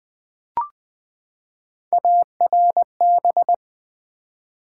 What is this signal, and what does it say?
Morse code, heard as a keyed tone, about 20 wpm, 700 Hz. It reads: ARB
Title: Build A Word - 3 Letters At A Time - 9-Letter Words - 20wpm